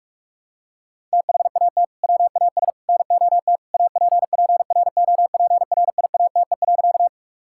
Morse code, heard as a keyed tone, about 42 wpm, 700 Hz. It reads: THAT WAS NOT APPROPRIATE.
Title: Sentences from Top 1000 Words - Review Part 5 - 45wpm